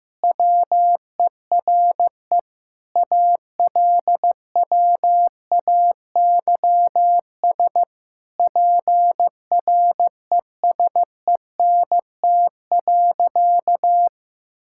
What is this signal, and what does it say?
Morse code, heard as a keyed tone, about 16 wpm, 700 Hz. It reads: WERE ALWAYS PRESENT.
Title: Sentences from Top 1000 Words - Review Part 5 - 15wpm